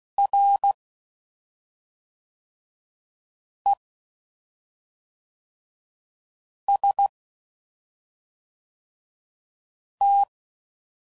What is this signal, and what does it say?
Morse code, heard as a keyed tone, about 16 wpm, 800 Hz. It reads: R E S T
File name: Basic Letter Groups By7Groups / EISH5 / EISH5@5 f15